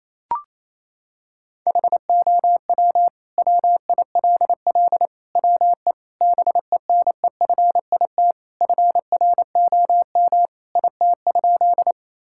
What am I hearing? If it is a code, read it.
Morse code, heard as a keyed tone, about 28 wpm, 690 Hz. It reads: HOW WILL WE BENEFIT FROM IT?
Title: Sentences from Top 1000 Words - Review Part 6 - 28wpm